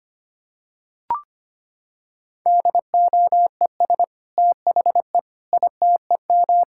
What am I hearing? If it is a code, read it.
Morse code, heard as a keyed tone, about 25 wpm, 700 Hz. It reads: DOES THE ITEM
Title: Sentences from Top 1000 Words - Review Part 6 - 25wpm